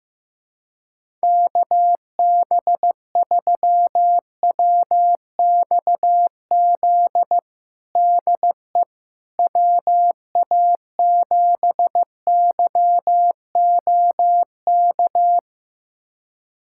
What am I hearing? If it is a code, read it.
Morse code, heard as a keyed tone, about 15 wpm, 700 Hz. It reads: KB3WXZ DE WA7YOK
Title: USA Callsign DE Callsign - 6 Characters - 15wpm